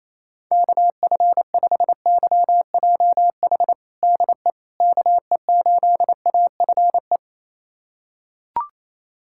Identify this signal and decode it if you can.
Morse code, heard as a keyed tone, about 28 wpm, 700 Hz. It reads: KF5YJH DE KE8AFE
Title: USA Callsign DE Callsign - 6 Characters - 28wpm